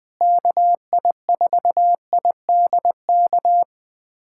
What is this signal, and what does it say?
Morse code, heard as a keyed tone, about 21 wpm, 700 Hz. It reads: KI4IDK
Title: USA Callsign DE Callsign - 6 Characters - 20wpm